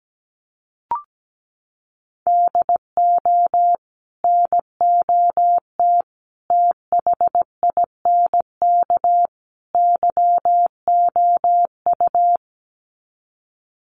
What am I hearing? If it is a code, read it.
Morse code, heard as a keyed tone, about 17 wpm, 700 Hz. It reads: DO NOT THINK YOU
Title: Sets of 4 Words 17wpm